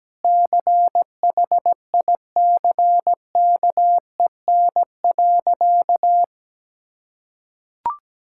Morse code, heard as a keyed tone, about 17 wpm, 700 Hz. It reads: CHICKEN.
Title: Sentences from Top 1000 Words - Review Part 5 - 17wpm